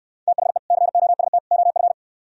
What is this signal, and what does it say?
Morse code, heard as a keyed tone, about 68 wpm, 700 Hz. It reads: THE COST OF